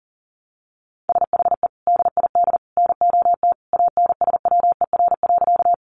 Morse code, heard as a keyed tone, about 37 wpm, 700 Hz. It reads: SHE DID NOT ANSWER.
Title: Sentences from Top 600 Words 40wpm